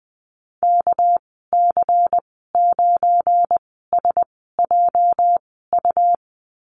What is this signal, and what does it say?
Morse code, heard as a keyed tone, about 20 wpm, spaced as for 14 wpm, 700 Hz. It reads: KC9SJU
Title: 238 Lesson 38g - Letter q - Call Signs 20wpm